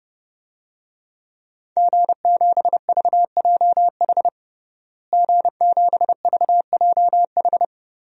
Morse code, heard as a keyed tone, about 30 wpm, 700 Hz. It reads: G7VJH G7VJH
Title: DXCC-100 Call Signs - 5 Characters - Repeated 3x - 30wpm